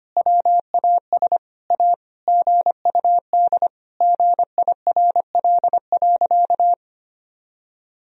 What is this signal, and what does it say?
Morse code, heard as a keyed tone, about 25 wpm, 700 Hz. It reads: WAS A GUD GIRL.
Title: Sentences from Top 400 Words and Abbreviations 25wpm